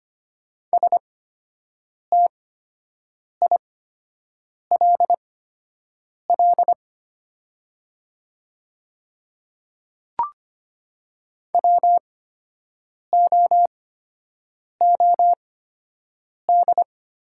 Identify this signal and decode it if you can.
Morse code, heard as a keyed tone, about 25 wpm, spaced as for 7 wpm, 690 Hz. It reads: STILL WOOD
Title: 8x Character Spacing - Top 300 Words - 25wpm